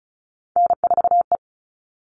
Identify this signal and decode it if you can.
Morse code, heard as a keyed tone, about 34 wpm, 700 Hz. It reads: N4E
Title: All Cut Numbers - 3-Digit 35wpm